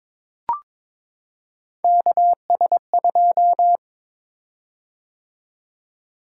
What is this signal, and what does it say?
Morse code, heard as a keyed tone, about 22 wpm, 700 Hz. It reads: KS2
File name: practice / 22wpm / Three Letter-Number Combinations - 22wpm